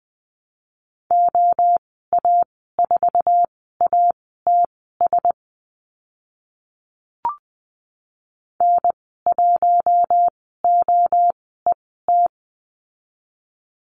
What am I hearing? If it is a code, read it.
Morse code, heard as a keyed tone, about 20 wpm, spaced as for 14 wpm, 700 Hz. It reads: OA4ATS N1OET
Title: 076 Lesson 13g - Letter L - Call Signs 20wpm